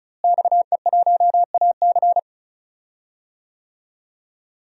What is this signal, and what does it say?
Morse code, heard as a keyed tone, about 34 wpm, 700 Hz. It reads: XE1AC
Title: DXCC-100 Call Signs - 5 Characters - Repeated 3x - 35wpm